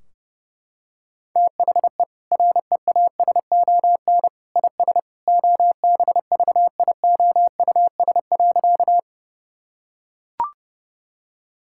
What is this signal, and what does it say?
Morse code, heard as a keyed tone, about 30 wpm, 700 Hz. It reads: THE REASON IS OBVIOUS.
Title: Sentences from Top 1600 Words 30wpm